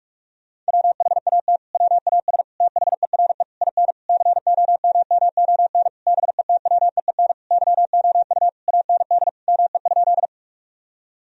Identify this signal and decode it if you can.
Morse code, heard as a keyed tone, about 46 wpm, 700 Hz. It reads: WHAT WAS THERE IN COMMON BETWEEN YOU AND ME?
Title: Sentences from Top 1000 Words - Review Part 3 - 45wpm